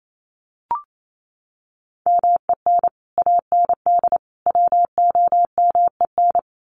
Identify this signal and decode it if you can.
Morse code, heard as a keyed tone, about 28 wpm, 700 Hz. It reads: MEN AND WOMEN
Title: Sets of 3 Words 28wpm